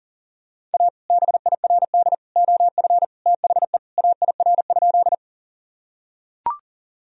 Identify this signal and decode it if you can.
Morse code, heard as a keyed tone, about 37 wpm, 700 Hz. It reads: A BIRD OF THE AIR?
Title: Sentences from Top 1000 Words - Review Part 2 - 40wpm